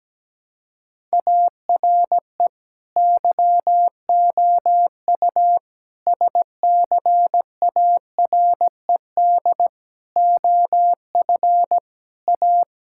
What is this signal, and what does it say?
Morse code, heard as a keyed tone, about 17 wpm, 700 Hz. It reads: ARE YOU SCARED OF A
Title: Sentences from Top 1000 Words - Review Part 5 - 17wpm